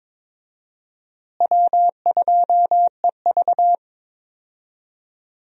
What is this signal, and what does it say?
Morse code, heard as a keyed tone, about 22 wpm, 700 Hz. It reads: W2EV
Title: USA Call Signs - 4 Characters - Repeated 2x - 22wpm